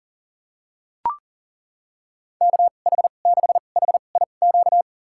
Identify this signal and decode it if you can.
Morse code, heard as a keyed tone, about 40 wpm, spaced as for 29 wpm, 700 Hz. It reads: KH6HIQ
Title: USA Call Signs - 6 Characters - 2x Character Spacing - 40wpm